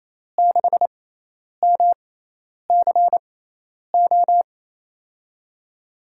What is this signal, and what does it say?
Morse code, heard as a keyed tone, about 28 wpm, spaced as for 10 wpm, 700 Hz. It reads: 6MCO
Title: USA Call Signs - 5 Characters - 6x Character Spacing - 28wpm